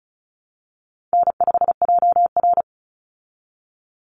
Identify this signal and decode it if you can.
Morse code, heard as a keyed tone, about 34 wpm, 700 Hz. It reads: N5JR
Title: Easy Call Signs Repeated 2x - 35wpm